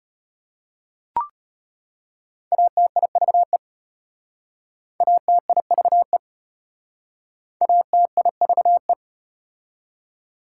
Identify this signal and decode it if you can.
Morse code, heard as a keyed tone, about 33 wpm, 700 Hz. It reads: ATIVE ATIVE ATIVE